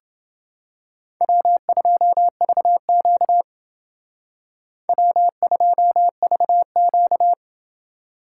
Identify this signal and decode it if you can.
Morse code, heard as a keyed tone, about 28 wpm, 700 Hz. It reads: W2VQ W2VQ